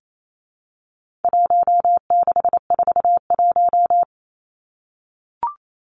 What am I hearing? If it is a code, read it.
Morse code, heard as a keyed tone, about 28 wpm, 700 Hz. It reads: 1641
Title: Numbers - Contest Style - 4-Digit 28wpm